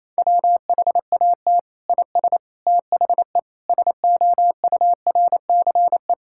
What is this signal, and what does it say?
Morse code, heard as a keyed tone, about 28 wpm, 700 Hz. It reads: WHAT IS THE SOURCE
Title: Sentences from Top 1000 Words - Review Part 5 - 28wpm